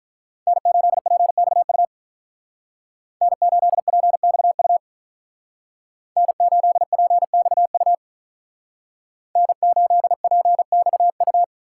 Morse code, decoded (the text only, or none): N8PXU N8PXU N8PXU N8PXU